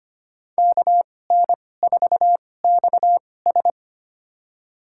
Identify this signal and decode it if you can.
Morse code, heard as a keyed tone, about 25 wpm, spaced as for 18 wpm, 700 Hz. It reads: KN4XS
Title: USA Call Signs - 2x Character Spacing - 25wpm